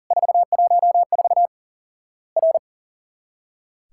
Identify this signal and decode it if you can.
Morse code, heard as a keyed tone, about 40 wpm, 700 Hz. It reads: 414 R
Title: Worked All Europe DX Contest - QTC Exchange - 40wpm